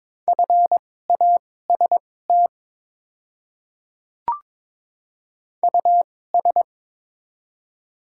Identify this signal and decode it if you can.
Morse code, heard as a keyed tone, about 22 wpm, spaced as for 16 wpm, 690 Hz. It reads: FAST US
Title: 2x Character Spacing - Top 500 Words - 22wpm